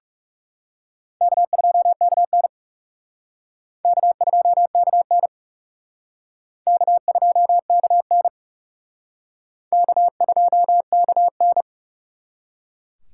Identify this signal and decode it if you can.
Morse code, heard as a keyed tone, about 37 wpm, 700 Hz. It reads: K2KN K2KN K2KN K2KN
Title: Speed Racing - USA 1x2 Call Signs 30wpm